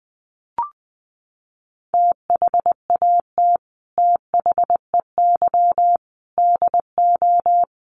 Morse code, heard as a keyed tone, about 20 wpm, 700 Hz. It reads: THAT THEY DO
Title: Sets of 5 Words Encore2 20wpm